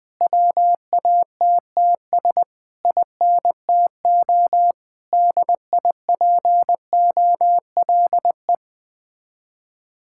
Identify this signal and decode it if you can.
Morse code, heard as a keyed tone, about 21 wpm, 700 Hz. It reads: WATTS INTO DIPOLE
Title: Speed-Racing Common QSO Phrases Encore - 20wpm